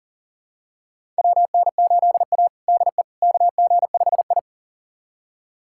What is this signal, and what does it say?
Morse code, heard as a keyed tone, about 40 wpm, 700 Hz. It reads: WN8A DE KG5I
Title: USA Callsign DE Callsign - 4 Characters - 40wpm